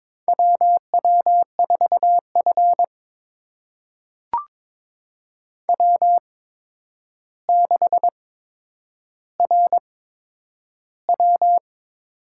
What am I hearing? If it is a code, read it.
Morse code, heard as a keyed tone, about 22 wpm, spaced as for 8 wpm, 700 Hz. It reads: W W4F W6RW
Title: USA Call Signs - 4 Characters - 8x Character Spacing - 22wpm